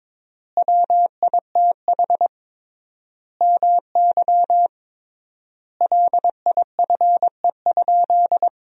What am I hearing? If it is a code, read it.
Morse code, heard as a keyed tone, about 22 wpm, 700 Hz. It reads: WITH MY LIFE?
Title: 3x Word-Spacing - Sentences from Top 300 Words - 22wpm